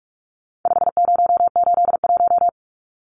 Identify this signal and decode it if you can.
Morse code, heard as a keyed tone, about 44 wpm, 700 Hz. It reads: 5081
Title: Numbers - 4-Digit 45wpm